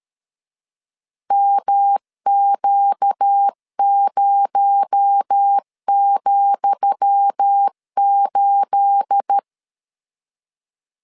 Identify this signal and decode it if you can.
Morse code, heard as a keyed tone, about 13 wpm, 800 Hz. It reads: MQ0,8
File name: LEZIONI / 10 SIGNS / 10D COMA 12wpm 2544